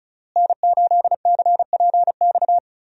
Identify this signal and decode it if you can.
Morse code, heard as a keyed tone, about 36 wpm, 700 Hz. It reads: N8CPX
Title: USA Call Signs - 5 Characters - Repeated 3x - 35wpm